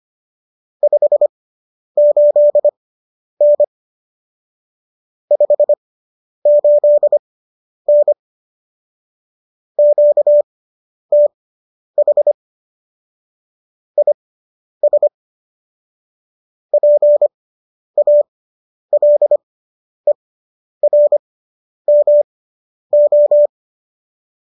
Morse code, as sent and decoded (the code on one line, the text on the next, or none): ..... ---.. -. / ..... ---.. -. / --.- - .... / .. ... / .--. .- .-.. .-. -- ---
58N 58N QTH IS PALERMO